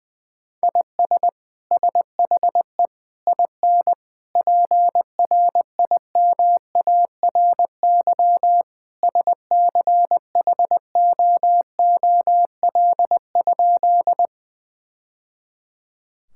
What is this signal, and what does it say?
Morse code, heard as a keyed tone, about 20 wpm, 700 Hz. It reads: IS SHE IN PRIMARY SCHOOL?
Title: Sentences from Top 1000 Words - Review Part 5 - 20wpm